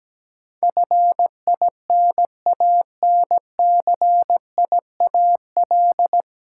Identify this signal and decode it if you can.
Morse code, heard as a keyed tone, about 17 wpm, 700 Hz. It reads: FINANCIAL
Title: Sentences from Top 1000 Words - Review Part 5 - New Word Drill - 17wpm